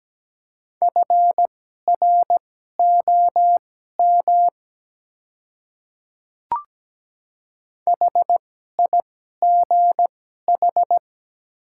Morse code, heard as a keyed tone, about 17 wpm, spaced as for 12 wpm, 700 Hz. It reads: FROM HIGH
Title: 2x Character Spacing - Top 200 Words - 17wpm